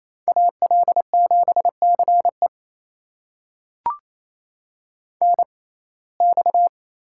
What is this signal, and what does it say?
Morse code, heard as a keyed tone, about 27 wpm, spaced as for 20 wpm, 700 Hz. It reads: AL7CE NX
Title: USA Call Signs - 6x Character Spacing - 28wpm